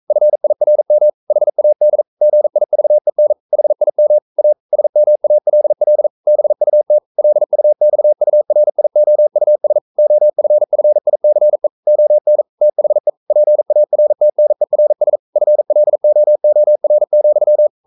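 Morse code, heard as a keyed tone, about 42 wpm, 600 Hz. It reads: FIRM HAD GIVEN HIM A SMALL BUT LUXURIOUS OFFICE ON THE PARTNERS FLOOR,
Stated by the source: Morse code practice at 41 wpm, from CW med Gnister.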